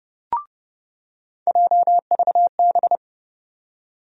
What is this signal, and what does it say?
Morse code, heard as a keyed tone, about 30 wpm, 700 Hz. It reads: JVB
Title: USA Call Sign Postfixes - 3 Letters 30wpm